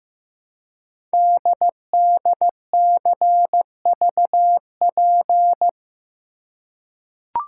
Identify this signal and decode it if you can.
Morse code, heard as a keyed tone, about 15 wpm, 700 Hz. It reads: DDCVP
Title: Code Groups - 5 Letters - 15wpm